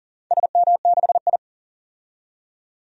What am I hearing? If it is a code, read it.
Morse code, heard as a keyed tone, about 39 wpm, 700 Hz. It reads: SM6I
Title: DXCC-100 Call Signs - 4 Characters - 40wpm